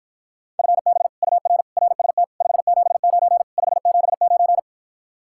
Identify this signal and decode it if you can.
Morse code, heard as a keyed tone, about 54 wpm, 700 Hz. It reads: FB UR RST 579 579